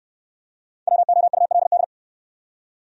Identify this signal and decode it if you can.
Morse code, heard as a keyed tone, about 69 wpm, 690 Hz. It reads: WOULD